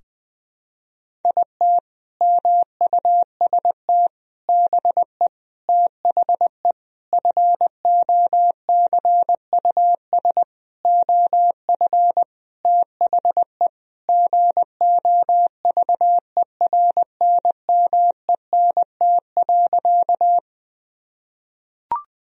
Morse code, heard as a keyed tone, about 21 wpm, 700 Hz. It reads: IT MUST BE THE FOCUS OF THE GOVERNMENT.
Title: Sentences from Top 1000 Words - Review Part 4 - 20wpm